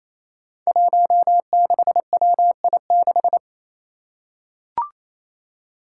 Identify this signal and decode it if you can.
Morse code, heard as a keyed tone, about 28 wpm, 700 Hz. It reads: 16WI6
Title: Five Challenging Letter-Number Combinations 28wpm